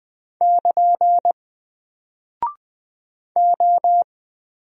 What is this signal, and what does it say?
Morse code, heard as a keyed tone, about 20 wpm, 700 Hz. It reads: <KN> O
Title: Single Character (Letter:Number:Punctuation:Prosign) 20wpm